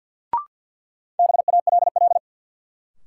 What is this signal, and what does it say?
Morse code, heard as a keyed tone, about 52 wpm, 700 Hz. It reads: BALL
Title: Sentences from Top 900 Words - New Word Drill - 50wpm